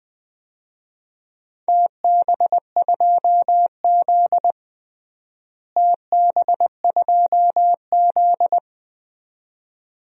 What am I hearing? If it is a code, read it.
Morse code, heard as a keyed tone, about 20 wpm, 700 Hz. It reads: TB2Z TB2Z
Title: DXCC-100 Call Signs - 4 Characters - Repeated 2x - 20wpm